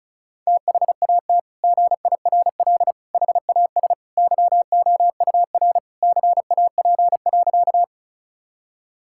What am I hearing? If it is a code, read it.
Morse code, heard as a keyed tone, about 36 wpm, 700 Hz. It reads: THAT GIRL HAS YOUR CAP.
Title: Sentences from Top 1600 Words 35wpm